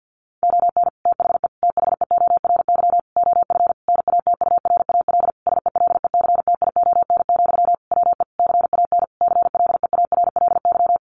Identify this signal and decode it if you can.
Morse code, heard as a keyed tone, about 50 wpm, 700 Hz. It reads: ON THE THEORY OF NATURAL SELECTION, WE CAN CLEARLY
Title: Sentences from Top 1400 Words 50wpm